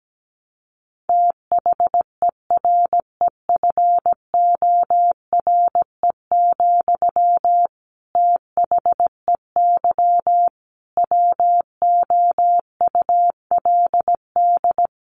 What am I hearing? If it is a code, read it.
Morse code, heard as a keyed tone, about 17 wpm, 700 Hz. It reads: THEREFORE, THEY WOULD
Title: Sentences from Top 800 Words 17wpm